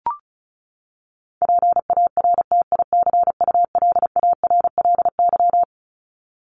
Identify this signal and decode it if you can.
Morse code, heard as a keyed tone, about 34 wpm, 700 Hz. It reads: PARTICULARLY
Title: LY Suffix - Top 100 Words with Base Words 35wpm